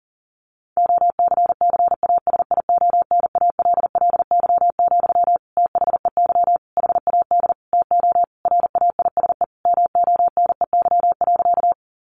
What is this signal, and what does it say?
Morse code, heard as a keyed tone, about 40 wpm, 700 Hz. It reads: OCCASIONALLY, THEY HAD TO RAISE MONEY.
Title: Sentences from Top 1400 Words 40wpm